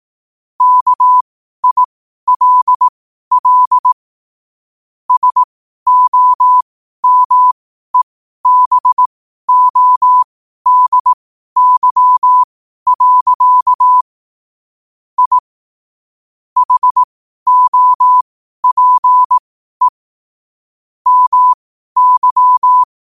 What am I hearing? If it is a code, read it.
Morse code, heard as a keyed tone, about 18 wpm, spaced as for 12 wpm, 1000 Hz. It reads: KILL SOMEBODY. I HOPE MY
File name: qotdfiles / 12wpm